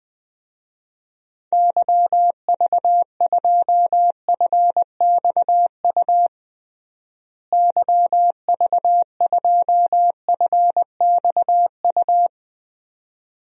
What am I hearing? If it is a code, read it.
Morse code, heard as a keyed tone, about 20 wpm, 700 Hz. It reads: YV2FXU YV2FXU
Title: DXCC-100 Call Signs - 6 Characters - Repeated 2x - 20wpm